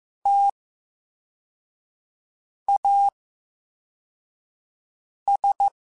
Morse code, heard as a keyed tone, about 15 wpm, spaced as for 4 wpm, 800 Hz. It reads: TAS
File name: Special INGs-mp3s / Special INGs-mp3s / VerySlowToFaster Ing @13